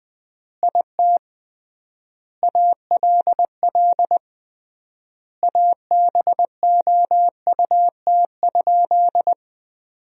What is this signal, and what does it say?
Morse code, heard as a keyed tone, about 20 wpm, 700 Hz. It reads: IT ALL ABOUT?
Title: 3x Word-Spacing - Sentences from Top 100 Words - 20wpm